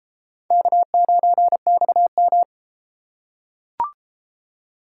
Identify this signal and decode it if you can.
Morse code, heard as a keyed tone, about 32 wpm, 700 Hz. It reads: K9XM